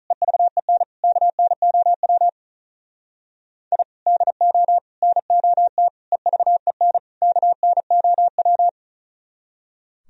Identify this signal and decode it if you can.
Morse code, heard as a keyed tone, about 46 wpm, 700 Hz. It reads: EVEN KNOW I DO NOT EVEN KNOW